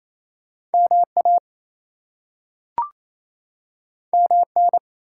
Morse code, heard as a keyed tone, about 28 wpm, 700 Hz. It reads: MA MN
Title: US States and Canadian Provinces 28wpm